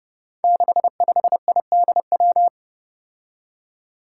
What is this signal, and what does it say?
Morse code, heard as a keyed tone, about 30 wpm, 700 Hz. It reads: 65IDW
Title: Five Challenging Letter-Number Combinations 30wpm